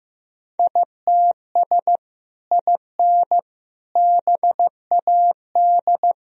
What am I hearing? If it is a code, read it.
Morse code, heard as a keyed tone, about 15 wpm, 700 Hz. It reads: ITS IN BAD
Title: Sentences from Top 1000 Words - Review Part 1 - 15wpm